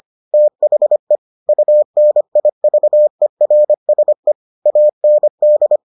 Morse code, heard as a keyed tone, about 25 wpm, 600 Hz. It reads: THE UNIVERSE AND